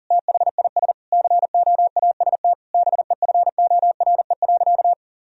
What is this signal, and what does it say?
Morse code, heard as a keyed tone, about 40 wpm, 700 Hz. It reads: THIS COAST BEFORE.
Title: Sentences from Top 1000 Words - Review Part 4 - 40wpm